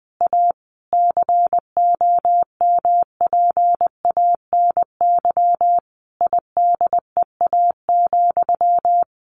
Morse code, heard as a keyed tone, about 20 wpm, 700 Hz. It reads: A COMPANY IDEA,
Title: Sentences from Top 600 Words 20wpm